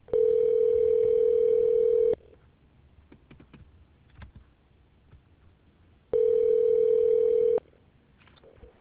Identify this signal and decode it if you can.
Ringback tone (North America), 440 + 480 Hz.